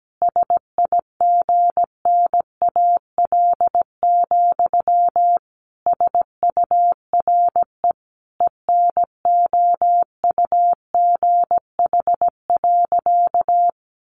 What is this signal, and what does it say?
Morse code, heard as a keyed tone, about 17 wpm, 700 Hz. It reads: SIGNAL, SURE ENOUGH.
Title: Sentences from Top 1400 Words 17wpm